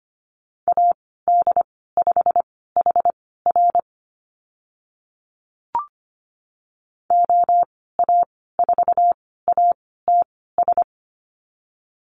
Morse code, heard as a keyed tone, about 25 wpm, spaced as for 16 wpm, 700 Hz. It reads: AD5HR OA4ATS
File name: Lesson 15g - Number 5 - Call Signs 25wpm